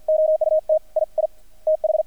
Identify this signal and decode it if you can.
Morse code, heard as a keyed tone, about 50 wpm, 630 Hz. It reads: OUT I I TH
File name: archived / CRSnet